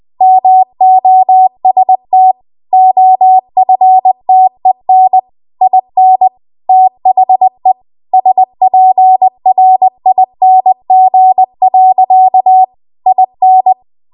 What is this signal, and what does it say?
Morse code, heard as a keyed tone, about 20 wpm, 750 Hz. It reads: MOST OFTEN IN THE SPRING. IN
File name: Archive / 20 WPM / 20WPM